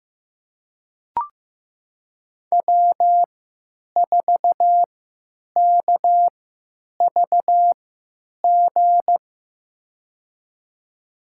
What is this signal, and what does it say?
Morse code, heard as a keyed tone, about 15 wpm, spaced as for 9 wpm, 700 Hz. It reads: W4KVG
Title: USA Call Signs - 5 Characters - 3x Character Spacing - 15wpm